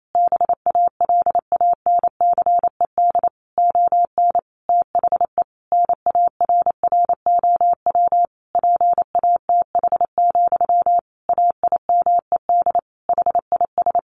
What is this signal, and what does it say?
Morse code, heard as a keyed tone, about 28 wpm, 700 Hz. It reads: BALANCED ON THE NARROW PATH, AIMED HIS
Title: A Sound Of Thunder 28wpm